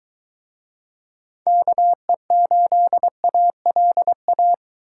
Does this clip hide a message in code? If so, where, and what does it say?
Yes: Morse code from 1.5 to 4.5 s, heard as a keyed tone, about 24 wpm, 700 Hz. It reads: KE8ALA